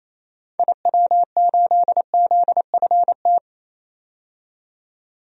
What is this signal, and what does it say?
Morse code, heard as a keyed tone, about 28 wpm, 700 Hz. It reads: IW8ZFT
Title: DXCC-100 Call Signs - 6 Characters - 28wpm